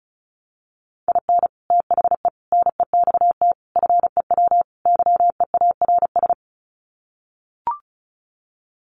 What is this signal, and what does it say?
Morse code, heard as a keyed tone, about 34 wpm, 700 Hz. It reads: IN THE NEXT FEW YEARS
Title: Sets of 5 Words Encore 35wpm